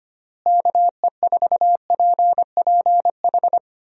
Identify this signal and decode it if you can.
Morse code, heard as a keyed tone, about 25 wpm, 700 Hz. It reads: KE4PPH
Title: USA Call Signs - 6 Characters - Repeated 3x - 25wpm